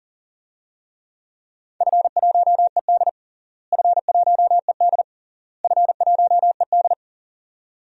Morse code, heard as a keyed tone, about 40 wpm, 700 Hz. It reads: F1ED F1ED F1ED
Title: DXCC-100 Call Signs - 4 Characters - Repeated 3x - 40wpm